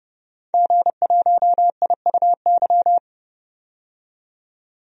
Morse code, heard as a keyed tone, about 32 wpm, 700 Hz. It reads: G1IUY
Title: DXCC-100 Call Signs - 5 Characters - 30wpm